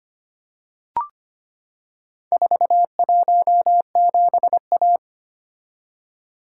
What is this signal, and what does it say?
Morse code, heard as a keyed tone, about 25 wpm, 700 Hz. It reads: 417A
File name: practice / 25wpm / Four Letter-Number Combinations - 25wpm